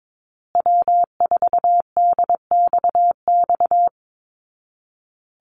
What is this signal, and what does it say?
Morse code, heard as a keyed tone, about 22 wpm, 700 Hz. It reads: W4DXX
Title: Call Signs Repeated 2x - 22wpm